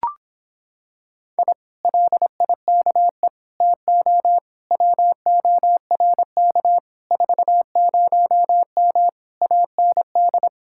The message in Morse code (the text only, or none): I LIKE TO WORK 40M AND